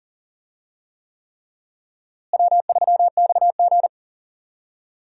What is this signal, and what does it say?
Morse code, heard as a keyed tone, about 40 wpm, 700 Hz. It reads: W3XG
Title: USA Call Signs - 4 Characters - Repeated 3x - 40wpm